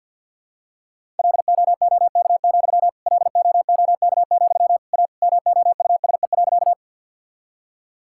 Morse code, heard as a keyed tone, about 50 wpm, 700 Hz. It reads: LOOK, LOOK, A MOUSE.
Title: Sentences from Top 1000 Words - Review Part 5 - 50wpm